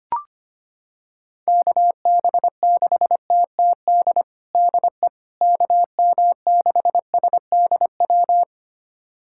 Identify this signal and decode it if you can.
Morse code, heard as a keyed tone, about 25 wpm, 700 Hz. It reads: KB6TTD DE KM6SDW
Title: USA Callsign DE Callsign - 6 Characters - 25wpm